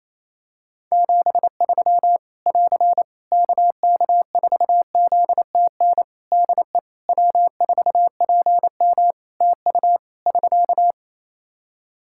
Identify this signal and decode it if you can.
Morse code, heard as a keyed tone, about 28 wpm, 700 Hz. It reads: 73 <AR> KK4ZTN DE W4PM TU <SK>
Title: Practice QSO - Exchange 4 - Ending - 28wpm